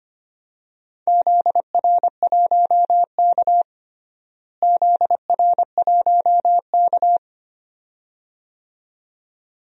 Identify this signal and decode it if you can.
Morse code, heard as a keyed tone, about 25 wpm, 700 Hz. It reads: ZR1K ZR1K
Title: DXCC-100 Call Signs - 4 Characters - Repeated 2x - 25wpm